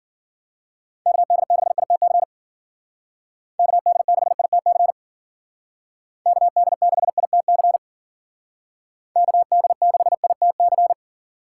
Morse code, heard as a keyed tone, about 52 wpm, 700 Hz. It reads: KD6ITC KD6ITC KD6ITC KD6ITC